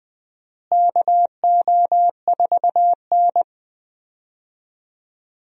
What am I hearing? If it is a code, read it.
Morse code, heard as a keyed tone, about 20 wpm, 700 Hz. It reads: KO4N
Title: USA Call Signs - 4 Characters - Repeated 3x - 20wpm